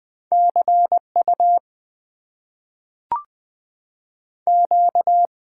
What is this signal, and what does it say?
Morse code, heard as a keyed tone, about 21 wpm, 700 Hz. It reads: CU Q
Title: Top 90 QSO Elements - 6x Character Spacing - 20wpm